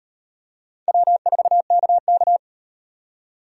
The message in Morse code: W4KK

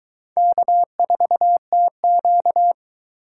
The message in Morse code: K4TQ

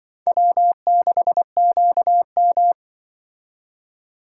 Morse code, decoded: W6QM

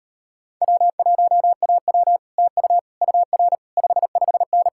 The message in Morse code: W1AW TU UR 55N

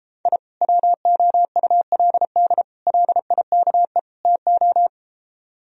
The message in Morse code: I WOULD LIKE TO

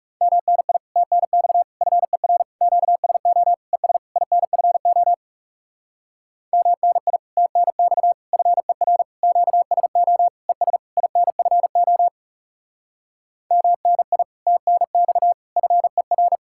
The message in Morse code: MNI TNX FER QSO ES INFO MNI TNX FER QSO ES INFO MNI TNX FER

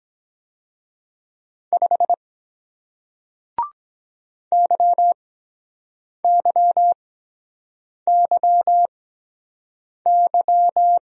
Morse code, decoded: 5 YYYY

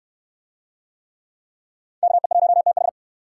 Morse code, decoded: KE1TH

